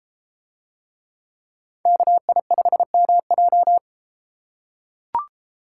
Morse code, heard as a keyed tone, about 33 wpm, 700 Hz. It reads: KI5MJ